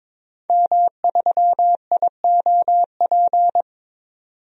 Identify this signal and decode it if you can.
Morse code, heard as a keyed tone, about 22 wpm, 700 Hz. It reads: M3IOP
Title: DXCC-100 Call Signs - 5 Characters - 22wpm